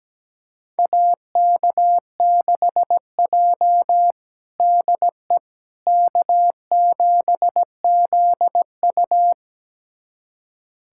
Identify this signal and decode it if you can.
Morse code, heard as a keyed tone, about 17 wpm, 700 Hz. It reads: AK6J DE K7ZU
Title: USA Callsign DE Callsign - 4 Characters - 17wpm